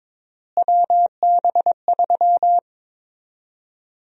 Morse code, heard as a keyed tone, about 22 wpm, 700 Hz. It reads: WB3
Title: USA Call Sign Prefixes - Repeated 2x - 22wpm